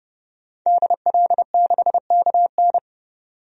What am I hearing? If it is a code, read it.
Morse code, heard as a keyed tone, about 30 wpm, 700 Hz. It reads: DL6KN
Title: DXCC-100 Call Signs - 5 Characters - 30wpm